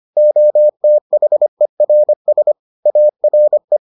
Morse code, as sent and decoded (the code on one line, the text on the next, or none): --- - .... .-. ... / .- .-. .
OTHERS ARE